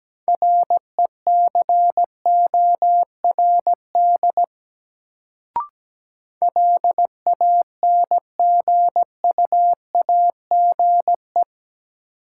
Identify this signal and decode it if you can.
Morse code, heard as a keyed tone, about 17 wpm, 700 Hz. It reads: RECORD LANGUAGE
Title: Sentences from Top 500 Words - New Word Drill - 17wpm